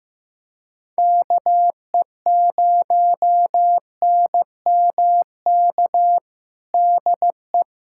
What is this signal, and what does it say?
Morse code, heard as a keyed tone, about 15 wpm, 700 Hz. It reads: KE0NMK DE
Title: USA Callsign DE Callsign 15wpm